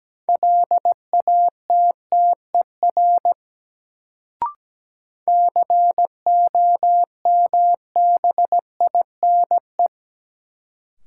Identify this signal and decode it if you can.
Morse code, heard as a keyed tone, about 17 wpm, 700 Hz. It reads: LATTER COMBINE
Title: Sentences from Top 1600 Words - New Word Drill Alt - 17wpm